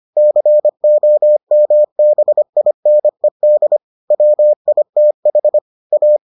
Morse code, heard as a keyed tone, about 25 wpm, 600 Hz. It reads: COMBINED WITH A